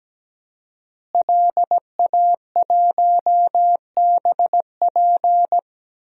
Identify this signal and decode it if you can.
Morse code, heard as a keyed tone, about 17 wpm, 700 Hz. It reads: LA1BP
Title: DXCC-100 Call Signs - 5 Characters - Repeated 3x - 17wpm